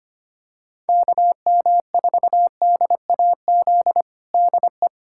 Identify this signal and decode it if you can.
Morse code, heard as a keyed tone, about 25 wpm, 700 Hz. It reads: KM4DAZ DE
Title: USA Callsign DE Callsign 25wpm